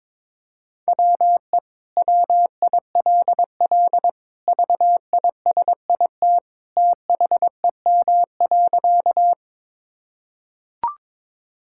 Morse code, heard as a keyed tone, about 22 wpm, 700 Hz. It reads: WE WILL VISIT THEM.
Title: Sentences from Top 1000 Words - Review Part 6 - 22wpm